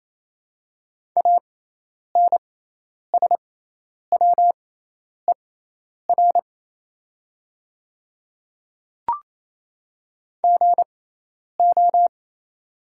Morse code, heard as a keyed tone, about 28 wpm, spaced as for 10 wpm, 700 Hz. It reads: ANSWER GO
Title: 6x Character Spacing - Top 300 Words - 28wpm